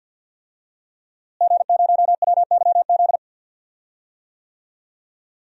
Morse code, heard as a keyed tone, about 46 wpm, 700 Hz. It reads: G0WYZ
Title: DXCC-100 Call Signs - 5 Characters - Repeated 3x - 50wpm